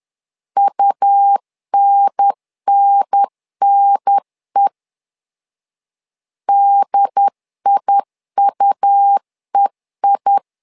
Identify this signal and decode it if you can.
Morse code, heard as a keyed tone, about 10 wpm, 800 Hz. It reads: UNNNE DIUEI
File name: LEZIONI / 2 UHDN / 2 UHDN 10wpm 1904